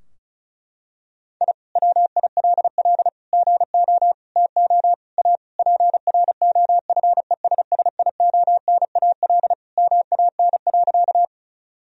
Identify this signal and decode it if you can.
Morse code, heard as a keyed tone, about 36 wpm, 700 Hz. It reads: I WILL GO TO A PROFESSIONAL MAN.
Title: Sentences from Top 1000 Words - Review Part 4 - 35wpm